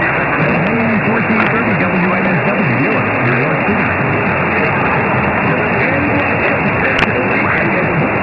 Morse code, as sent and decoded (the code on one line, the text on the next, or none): none